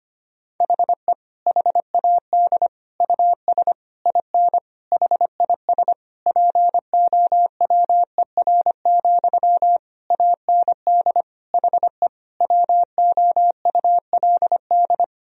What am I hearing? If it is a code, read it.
Morse code, heard as a keyed tone, about 25 wpm, 700 Hz. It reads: HE HAD US IN HIS POWER, AND HE WOULD
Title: Sentences from Top 1000 Words - Review Part 2 - 25wpm